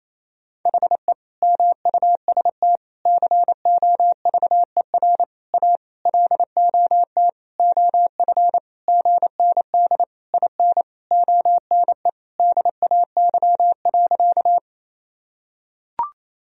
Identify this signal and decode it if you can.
Morse code, heard as a keyed tone, about 28 wpm, 700 Hz. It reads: HE MUST COVER A LOT OF GND IN ONE DAY.
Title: Sentences from Top 500 Words and Abbreviations 28wpm